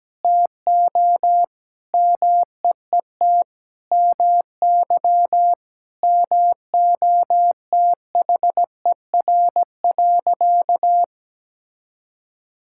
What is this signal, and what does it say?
Morse code, heard as a keyed tone, about 17 wpm, 700 Hz. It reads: TO MEET MY MOTHER.
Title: Sentences from Top 1000 Words - Review Part 6 - 17wpm